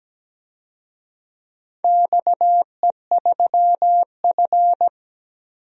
Morse code, heard as a keyed tone, about 17 wpm, 700 Hz. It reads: XE3F
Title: DXCC-100 Call Signs - 4 Characters - 17wpm